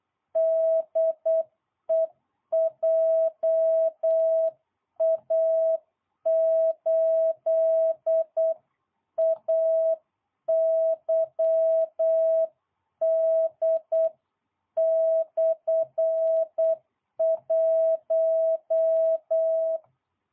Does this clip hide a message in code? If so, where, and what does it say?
Yes: Morse code from 0.4 to 19.8 s, heard as a keyed tone, about 8 wpm, 660 Hz. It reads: DE JA8AYD/1